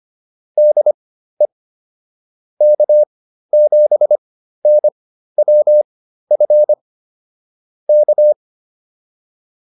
Morse code, heard as a keyed tone, about 25 wpm, spaced as for 13 wpm, 600 Hz. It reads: DE K7NWF K
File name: QSO209 13